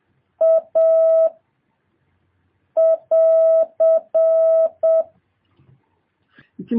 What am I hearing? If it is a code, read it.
Morse code, heard as a keyed tone, about 7 wpm, 650 Hz. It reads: A <AR>
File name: jh8cbh / 2023cw